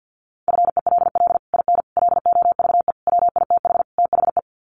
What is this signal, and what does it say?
Morse code, heard as a keyed tone, about 50 wpm, 700 Hz. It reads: FELL IN LOVE WITH THE
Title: Sets of 5 Words Encore2 50wpm